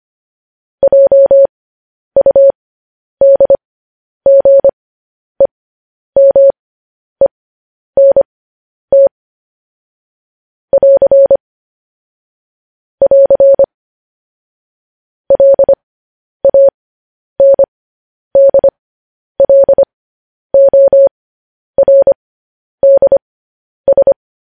Morse code, heard as a keyed tone, about 25 wpm, spaced as for 10 wpm, 550 Hz. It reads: JUDGEMENT <AR> <AR> LANDLORDS
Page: news-headlines-in-morse-code-at-10-wpm